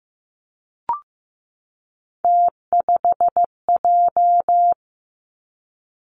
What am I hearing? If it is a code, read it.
Morse code, heard as a keyed tone, about 15 wpm, 700 Hz. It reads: T5J
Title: Three Challenging Letter-Number Combinations 15wpm